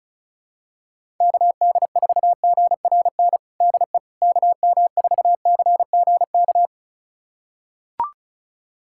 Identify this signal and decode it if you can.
Morse code, heard as a keyed tone, about 36 wpm, 700 Hz. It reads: KD4GRN DE KM4CGK
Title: USA Callsign DE Callsign - 6 Characters - 35wpm